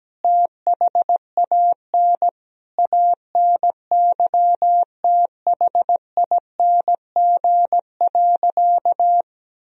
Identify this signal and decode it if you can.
Morse code, heard as a keyed tone, about 17 wpm, 700 Hz. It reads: THAN ANYTHING.
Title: Sentences from Top 1000 Words - Review Part 3 - 17wpm